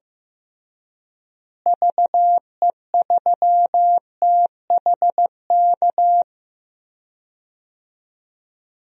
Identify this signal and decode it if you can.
Morse code, heard as a keyed tone, about 15 wpm, 700 Hz. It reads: VE3THK
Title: DXCC-100 Call Signs - 6 Characters - Repeated 3x - 15wpm